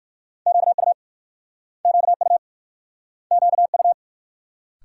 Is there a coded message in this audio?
Yes: Morse code from 0.5 to 3.9 s, heard as a keyed tone, about 56 wpm, 700 Hz. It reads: QU QU QU